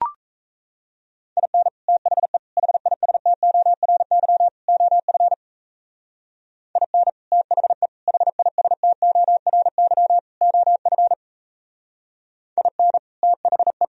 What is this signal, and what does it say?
Morse code, heard as a keyed tone, about 34 wpm, 700 Hz. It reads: IN THE HISTORY OF IN THE HISTORY OF IN THE
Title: Speed-Racing - Sets of 4 Words 28wpm